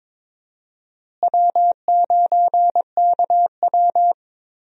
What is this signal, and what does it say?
Morse code, heard as a keyed tone, about 22 wpm, 700 Hz. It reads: W9KW
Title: USA Call Signs - 4 Characters - Repeated 3x - 22wpm